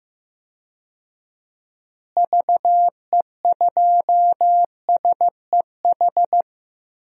Morse code, heard as a keyed tone, about 15 wpm, 700 Hz. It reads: VE2SEH
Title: DXCC-100 Call Signs - 6 Characters - Repeated 2x - 15wpm